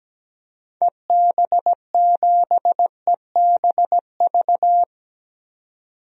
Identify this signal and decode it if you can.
Morse code, heard as a keyed tone, about 17 wpm, 700 Hz. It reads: EB7EBV
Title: DXCC-100 Call Signs - 6 Characters - 17wpm